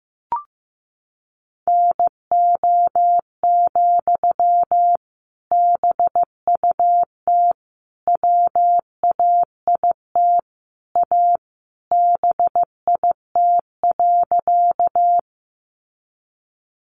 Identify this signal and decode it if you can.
Morse code, heard as a keyed tone, about 15 wpm, 700 Hz. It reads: NO, BUT WAIT A BIT.
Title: Sentences from Top 900 Words 15wpm